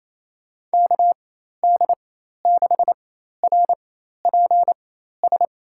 Morse code, heard as a keyed tone, about 27 wpm, spaced as for 13 wpm, 700 Hz. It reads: KD6RPS
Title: USA Call Signs - 6 Characters - 4x Character Spacing - 28wpm